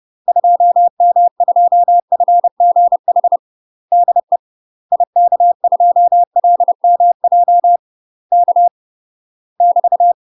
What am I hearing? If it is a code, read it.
Morse code, heard as a keyed tone, about 30 wpm, 700 Hz. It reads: 2M2FGH DE IK2LMJ K <BT>
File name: UK MASTER QSOs TRAINING / 30WPM / UK MASTER QSOs 0031